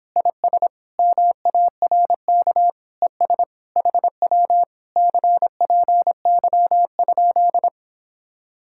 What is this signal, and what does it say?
Morse code, heard as a keyed tone, about 26 wpm, 700 Hz. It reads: IS MARK ES HW CPY?